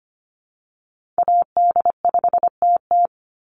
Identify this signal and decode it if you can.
Morse code, heard as a keyed tone, about 25 wpm, 700 Hz. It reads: AD5TT
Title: Call Signs - 25wpm